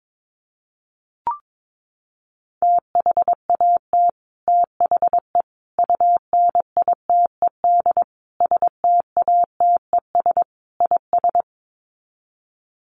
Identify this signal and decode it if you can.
Morse code, heard as a keyed tone, about 22 wpm, 700 Hz. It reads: THAT THE UNITED STATES IS
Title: Sets of 5 Words Encore 22wpm